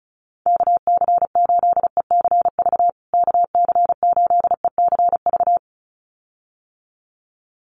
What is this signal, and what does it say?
Morse code, heard as a keyed tone, about 34 wpm, 700 Hz. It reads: KC8ECV KC8ECV
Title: USA Call Signs Repeated 2x 35wpm